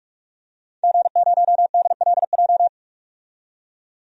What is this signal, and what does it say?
Morse code, heard as a keyed tone, about 41 wpm, 700 Hz. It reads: G0DLJ
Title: DXCC-100 Call Signs - 5 Characters - 45wpm